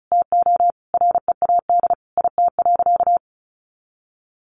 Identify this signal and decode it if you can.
Morse code, heard as a keyed tone, about 34 wpm, 700 Hz. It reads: TO READ IT.
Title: Sentences from Top 500 Words 35wpm